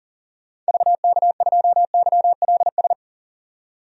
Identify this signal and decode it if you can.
Morse code, heard as a keyed tone, about 37 wpm, 700 Hz. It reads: VK2YLS
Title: DXCC-100 Call Signs - 6 Characters - 40wpm